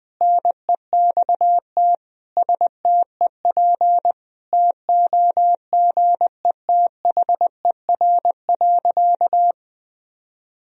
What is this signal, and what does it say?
Morse code, heard as a keyed tone, about 21 wpm, 700 Hz. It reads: NEXT STEP TOGETHER.
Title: Sentences from Top 1000 Words - Review Part 2 - 20wpm